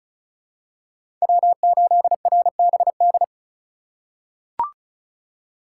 Morse code, heard as a keyed tone, about 34 wpm, 700 Hz. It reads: W8RBD
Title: USA Call Signs - 5 Characters - 2x Character Spacing - 35wpm